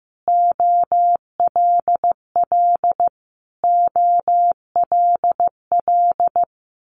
Morse code, heard as a keyed tone, about 15 wpm, 700 Hz. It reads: OLL OLL
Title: Multi-letter Phonemes Repeated 3x 15wpm